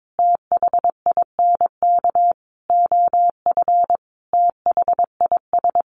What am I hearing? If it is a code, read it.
Morse code, heard as a keyed tone, about 22 wpm, 700 Hz. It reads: THINK OF THIS
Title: Sentences from Top 1200 Words 22wpm